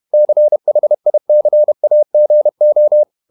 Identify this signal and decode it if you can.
Morse code, heard as a keyed tone, about 31 wpm, 600 Hz. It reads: CHICAGO
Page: Never 31wpm | CW med Gnister